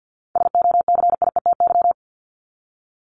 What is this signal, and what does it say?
Morse code, heard as a keyed tone, about 50 wpm, 700 Hz. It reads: SOCIETY
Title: Top 501-750 Words 50wpm